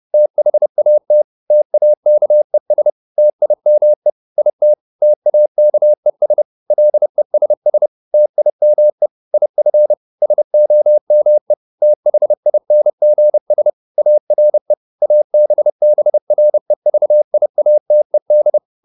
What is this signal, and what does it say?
Morse code, heard as a keyed tone, about 30 wpm, 600 Hz. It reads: THAT TAKES TIME IT TAKES LESS TIME IF SOME THINGS ARE ABBREVIATED